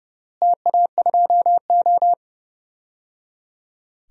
Morse code, heard as a keyed tone, about 30 wpm, 700 Hz. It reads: TA2O
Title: DXCC-100 Call Signs - 4 Characters - Repeated 3x - 30wpm